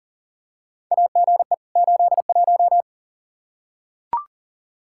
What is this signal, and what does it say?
Morse code, heard as a keyed tone, about 40 wpm, 700 Hz. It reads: AGE 81
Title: Practice QSO - Exchange 3 - More Info - 40wpm